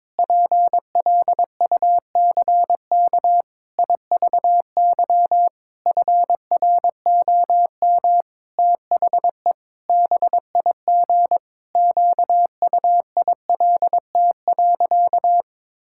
Morse code, heard as a keyed tone, about 22 wpm, 700 Hz. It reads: PLUCK IVY FROM THE BIG QUILT.
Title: Pangram Challenge v2 - 22wpm